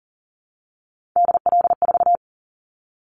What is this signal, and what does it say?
Morse code, heard as a keyed tone, about 40 wpm, 700 Hz. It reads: DL4
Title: Callsign Prefixes for Top 25 DXCC - Repeated 3x - 40wpm